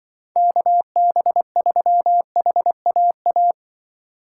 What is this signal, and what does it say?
Morse code, heard as a keyed tone, about 24 wpm, 700 Hz. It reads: KB3HAA
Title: Speed Racing - USA 2x3 Call Signs 20wpm